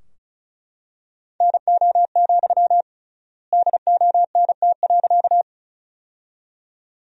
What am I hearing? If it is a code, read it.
Morse code, heard as a keyed tone, about 36 wpm, 700 Hz. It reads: NO, DONT.
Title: 3x Word-Spacing - Sentences from Top 300 Words - 35wpm